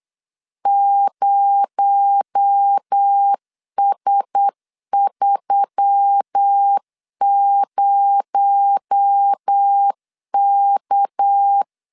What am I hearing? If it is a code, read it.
Morse code, heard as a keyed tone, about 8 wpm, 800 Hz. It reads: 0S30K
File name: LEZIONI / 6 PKX0 / 6 PKX0 8wpm 3341